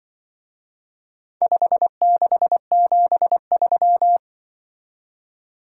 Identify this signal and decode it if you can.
Morse code, heard as a keyed tone, about 25 wpm, 700 Hz. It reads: 5673
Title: Speed-Racing - Four Letter-Number Combinations - 20wpm